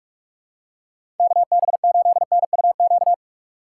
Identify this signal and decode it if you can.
Morse code, heard as a keyed tone, about 44 wpm, 700 Hz. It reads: KB8NUQ
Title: USA Call Signs - 6 Characters - 6x Character Spacing - 45wpm